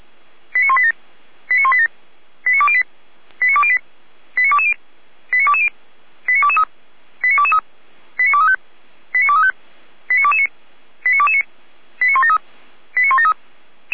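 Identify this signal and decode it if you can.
Five-tone selective call (ZVEI), heard as quick runs of five tones, tones of about 70 ms. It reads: 89187 89187 89208 89208 89220 89220 89223 89223 89235 89235 89229 89229 89173 89173
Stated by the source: ZVEI five-tone call 89240 received on SDR, feuerwehr-orlamuende fire brigade.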